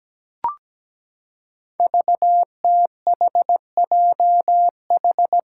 Morse code, heard as a keyed tone, about 17 wpm, 700 Hz. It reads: VTHJH
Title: Five Challenging Letter-Number Combinations 17wpm